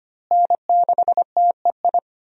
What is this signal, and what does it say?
Morse code, heard as a keyed tone, about 25 wpm, 700 Hz. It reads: N6TEI
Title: USA Call Signs - 5 Characters - Repeated 2x - 25wpm